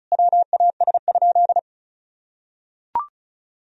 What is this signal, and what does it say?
Morse code, heard as a keyed tone, about 34 wpm, 700 Hz. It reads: WAS?
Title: Sentences from Top 200 Words and Abbreviations 35wpm